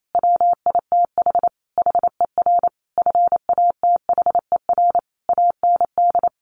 Morse code, heard as a keyed tone, about 28 wpm, 700 Hz. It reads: WITH HER FATHER AND
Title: Sentences from Top 1100 Words 28wpm